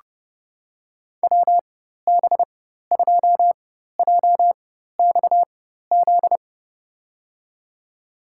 Morse code, heard as a keyed tone, about 30 wpm, spaced as for 14 wpm, 700 Hz. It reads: WB2JXZ
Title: USA Call Signs - 6 Characters - 4x Character Spacing - 30wpm